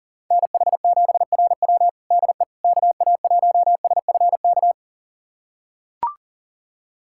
Morse code, heard as a keyed tone, about 40 wpm, 700 Hz. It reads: NH7RW DE KA1SFK
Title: USA Callsign DE Callsign 40wpm